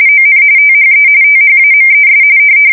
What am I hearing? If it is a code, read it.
RTTY, about 45.45 baud. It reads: ON THE THAMES
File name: fsk170 / usnavy2